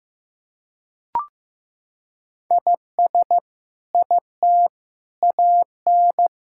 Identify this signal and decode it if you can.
Morse code, heard as a keyed tone, about 15 wpm, 700 Hz. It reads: IS IT AN
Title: Sentences from Top 1000 Words - Review Part 6 - 15wpm